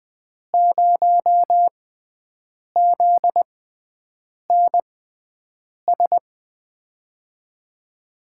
Morse code, heard as a keyed tone, about 20 wpm, spaced as for 7 wpm, 700 Hz. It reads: 0ZNS
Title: USA Call Signs - 5 Characters - 6x Character Spacing - 20wpm